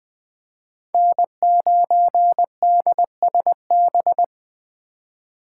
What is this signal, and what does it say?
Morse code, heard as a keyed tone, about 20 wpm, 700 Hz. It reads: N9DSB
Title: USA Call Signs - 5 Characters - 4x Character Spacing - 20wpm